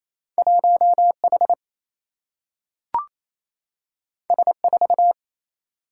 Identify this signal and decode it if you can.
Morse code, heard as a keyed tone, about 28 wpm, 700 Hz. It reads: 1H S4
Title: Single-Letter Single-Number 28wpm